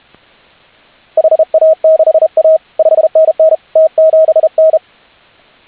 Morse code, heard as a keyed tone, about 34 wpm, 590 Hz. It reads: HA6A 5NN T7N